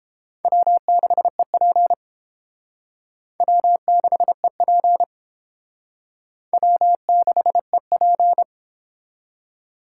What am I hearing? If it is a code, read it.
Morse code, heard as a keyed tone, about 31 wpm, 700 Hz. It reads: W6EP W6EP W6EP